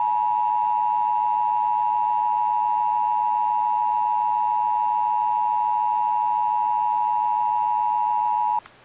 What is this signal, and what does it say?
Emergency Alert System (SAME) tones (the 853 and 960 Hz attention tone). It is heard from 0.3 to 8.6 s.
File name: eas / KBBX / KBBX